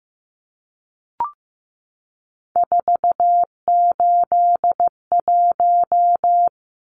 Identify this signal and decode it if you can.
Morse code, heard as a keyed tone, about 15 wpm, 700 Hz. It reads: 481
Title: Numbers - 3-Digit 15wpm